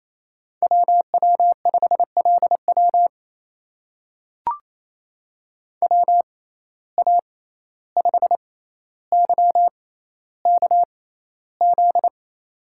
Morse code, heard as a keyed tone, about 28 wpm, spaced as for 14 wpm, 700 Hz. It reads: WW5LW WA5YKZ
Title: USA Call Signs - 6x Character Spacing - 28wpm